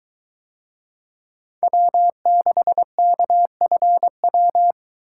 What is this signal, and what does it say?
Morse code, heard as a keyed tone, about 23 wpm, 700 Hz. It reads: W6KFW